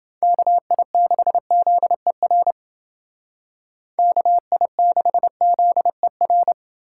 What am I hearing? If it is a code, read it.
Morse code, heard as a keyed tone, about 28 wpm, 700 Hz. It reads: KI6ZER KI6ZER